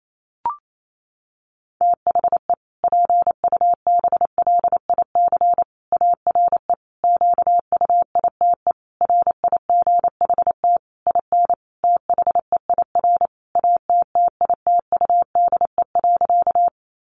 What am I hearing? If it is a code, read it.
Morse code, heard as a keyed tone, about 28 wpm, 700 Hz. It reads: THE PUBLIC ARE QUITE RIGHT IN THEIR ATTITUDE.
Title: Sentences from Top 1300 Words 28wpm